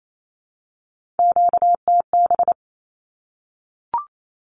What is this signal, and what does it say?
Morse code, heard as a keyed tone, about 28 wpm, 700 Hz. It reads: QTB
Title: Q-Codes 28wpm